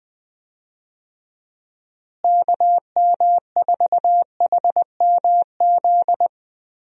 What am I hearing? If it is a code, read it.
Morse code, heard as a keyed tone, about 20 wpm, 700 Hz. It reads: KM4HMZ
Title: USA Call Signs - 6 Characters - Repeated 3x - 20wpm